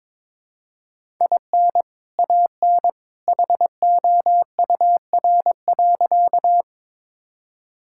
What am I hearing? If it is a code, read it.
Morse code, heard as a keyed tone, about 22 wpm, 700 Hz. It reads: IN AN HOUR.
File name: practice / 22wpm / Sentences with 2-Word Phrases - 22wpm